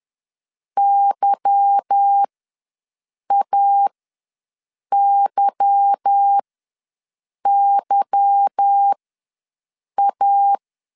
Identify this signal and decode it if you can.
Morse code, heard as a keyed tone, about 10 wpm, spaced as for 6 wpm, 800 Hz. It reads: YAYYA
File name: LEZIONI / 8 YJ45 / 8 Y ws 10wpm 3952